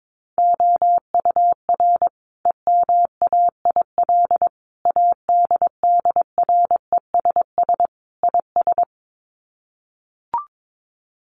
Morse code, heard as a keyed tone, about 22 wpm, 700 Hz. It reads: OUR EMAIL ADDRESS IS
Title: Sets of 5 Words Encore2 22wpm